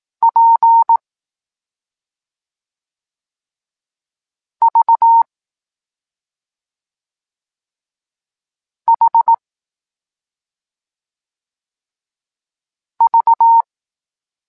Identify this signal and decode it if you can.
Morse code, heard as a keyed tone, about 18 wpm, 920 Hz. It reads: P V H V
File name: Tekst-26 1